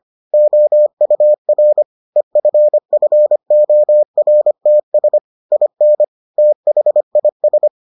Morse code, heard as a keyed tone, about 25 wpm, 600 Hz. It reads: OUR EFFORTS IN THIS